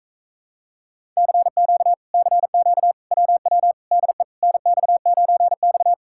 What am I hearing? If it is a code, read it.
Morse code, heard as a keyed tone, about 37 wpm, 700 Hz. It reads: CQ CQ WW DE NX9X